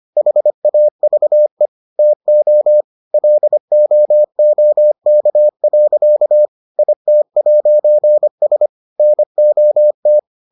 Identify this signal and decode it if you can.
Morse code, heard as a keyed tone, about 25 wpm, 600 Hz. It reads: HAVE TO LOOK. IT'S NOT